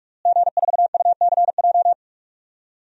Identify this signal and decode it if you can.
Morse code, heard as a keyed tone, about 46 wpm, 700 Hz. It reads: G4UCJ
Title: DXCC-100 Call Signs - 5 Characters - Repeated 2x - 45wpm